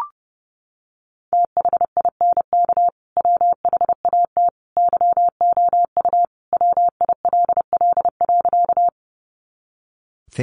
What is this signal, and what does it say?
Morse code, heard as a keyed tone, about 28 wpm, 700 Hz. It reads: THINK WHAT YOU WILL.
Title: Sentences from Top 200 Words 30wpm